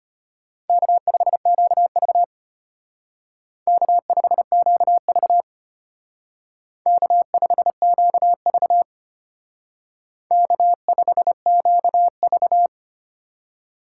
Morse code, heard as a keyed tone, about 33 wpm, 700 Hz. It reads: K5QV K5QV K5QV K5QV